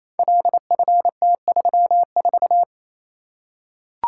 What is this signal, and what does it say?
Morse code, heard as a keyed tone, about 27 wpm, 700 Hz. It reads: LFT34
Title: Code Groups - 5 Letter:Numbers - 28wpm